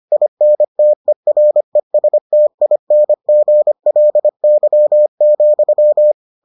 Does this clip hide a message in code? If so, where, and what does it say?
Yes: Morse code from 0.1 to 6.1 s, heard as a keyed tone, about 25 wpm, 600 Hz. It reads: INTERESTINGLY,